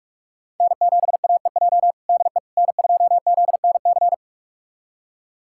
Morse code, heard as a keyed tone, about 46 wpm, 700 Hz. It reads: N7AEJ DE N2ZNC
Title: USA Callsign DE Callsign 45wpm